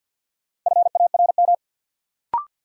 Morse code, heard as a keyed tone, about 53 wpm, 700 Hz. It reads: FARM
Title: Sentences from Top 1000 Words - Review Part 2 - New Word Drill - 50wpm